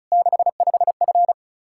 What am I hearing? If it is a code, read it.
Morse code, heard as a keyed tone, about 36 wpm, 700 Hz. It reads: 65F
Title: Practice QSO - Exchange 2 - Info - 35wpm